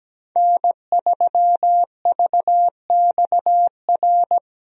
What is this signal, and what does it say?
Morse code, heard as a keyed tone, about 17 wpm, 700 Hz. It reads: N3VXR
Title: USA Call Signs - 5 Characters - Repeated 3x - 17wpm